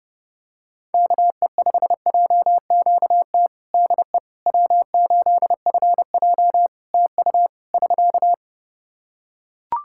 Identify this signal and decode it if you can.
Morse code, heard as a keyed tone, about 30 wpm, 700 Hz. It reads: KE5JQT DE W8FJ TU <SK>
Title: Practice QSO - Exchange 4 - Ending - 30wpm